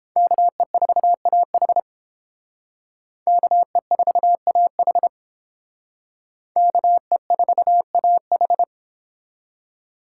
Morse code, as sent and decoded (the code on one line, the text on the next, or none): -.- . ....- .- .... / -.- . ....- .- .... / -.- . ....- .- ....
KE4AH KE4AH KE4AH